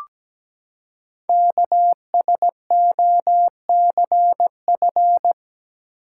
Morse code, heard as a keyed tone, about 17 wpm, 700 Hz. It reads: KSOCF
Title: Code Groups - 5 Letter:Numbers - 17wpm